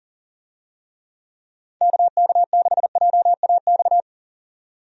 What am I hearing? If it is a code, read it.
Morse code, heard as a keyed tone, about 40 wpm, 700 Hz. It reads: KK6JAX
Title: USA Call Signs - 6 Characters - 40wpm